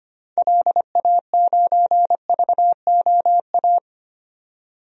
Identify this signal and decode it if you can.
Morse code, heard as a keyed tone, about 25 wpm, 690 Hz. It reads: LA9VOA
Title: DXCC-100 Call Signs - 6 Characters - 25wpm